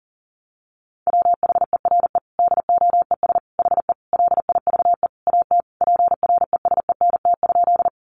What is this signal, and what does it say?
Morse code, heard as a keyed tone, about 40 wpm, 700 Hz. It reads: WHERE DOES HE LIVE AT PRESENT?
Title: Sentences from Top 800 Words 40wpm